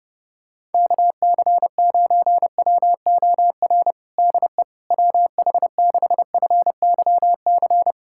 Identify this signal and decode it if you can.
Morse code, heard as a keyed tone, about 32 wpm, 700 Hz. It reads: KC9WOR DE WH6FYC
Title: USA Callsign DE Callsign 30wpm